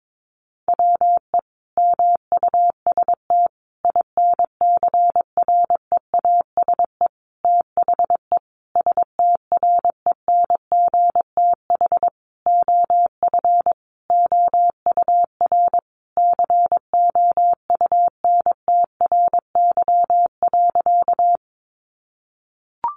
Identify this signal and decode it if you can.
Morse code, heard as a keyed tone, about 22 wpm, 700 Hz. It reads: WE MUST INCREASE THE STRENGTH OF OUR COUNTRY.
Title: Sentences from Top 1300 Words 22wpm